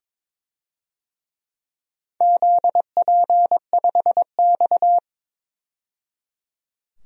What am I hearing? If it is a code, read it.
Morse code, heard as a keyed tone, about 22 wpm, 700 Hz. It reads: ZP5X
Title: DXCC-100 Call Signs - 4 Characters - 22wpm